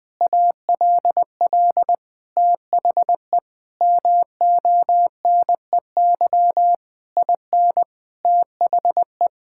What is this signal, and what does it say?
Morse code, heard as a keyed tone, about 20 wpm, 700 Hz. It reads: ALL THE MONEY IN THE
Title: Sentences from Top 1000 Words - Review Part 6 - 20wpm